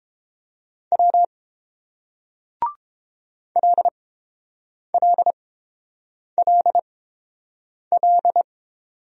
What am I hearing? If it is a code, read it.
Morse code, heard as a keyed tone, about 29 wpm, spaced as for 7 wpm, 690 Hz. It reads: W LLLL